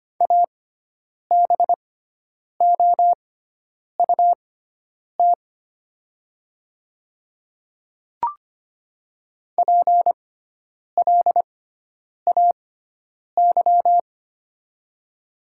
Morse code, heard as a keyed tone, about 25 wpm, spaced as for 9 wpm, 700 Hz. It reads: ABOUT PLAY
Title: 6x Character Spacing - Top 500 Words - 25wpm